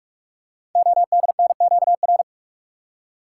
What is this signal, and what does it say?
Morse code, heard as a keyed tone, about 44 wpm, 690 Hz. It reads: ODNQR
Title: Code Groups - 5 Letters - 45wpm